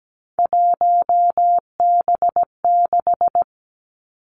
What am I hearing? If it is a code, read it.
Morse code, heard as a keyed tone, about 17 wpm, 700 Hz. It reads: 1B6
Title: Three Challenging Letter-Number Combinations 17wpm